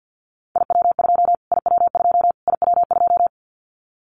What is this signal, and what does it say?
Morse code, heard as a keyed tone, about 50 wpm, 700 Hz. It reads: IW2 IW2 IW2
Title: Callsign Prefixes for Top 25 DXCC - Repeated 3x - 50wpm